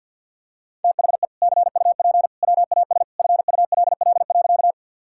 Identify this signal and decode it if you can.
Morse code, heard as a keyed tone, about 50 wpm, 700 Hz. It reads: THE CUP WAS FULL.
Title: Sentences from Top 1600 Words 50wpm